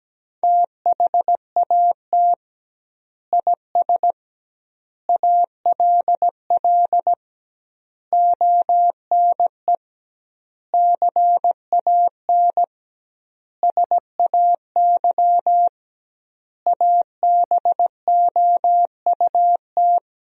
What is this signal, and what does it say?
Morse code, heard as a keyed tone, about 17 wpm, 700 Hz. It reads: THAT IS ALL ONE CAN SAY ABOUT
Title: 2x Word-Spacing - Sentences from Top 200 Words - 17wpm